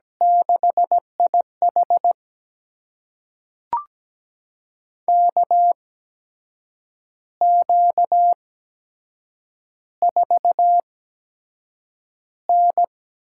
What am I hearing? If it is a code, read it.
Morse code, heard as a keyed tone, about 17 wpm, spaced as for 7 wpm, 700 Hz. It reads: 6IH KQ4N
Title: USA Call Signs - 4 Characters - 8x Character Spacing - 17wpm